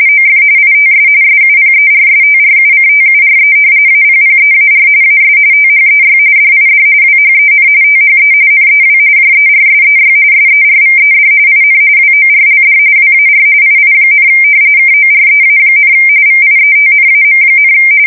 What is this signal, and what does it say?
RTTY, about 45.45 baud. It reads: OF ITS SUPERSTRUCTURE APPEAR AS THEY DID WHEN THE NEW JERSEY SERVED IN WORLD WAR II, KOREA, VIETNAM,
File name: fsk170 / battleshipnj